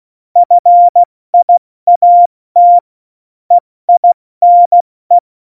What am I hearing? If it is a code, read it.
Morse code, heard as a keyed tone, about 16 wpm, spaced as for 14 wpm, 700 Hz. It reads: FIAT EINE
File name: Morsekurs / Witze / Witz 04 August 2020, 70 F80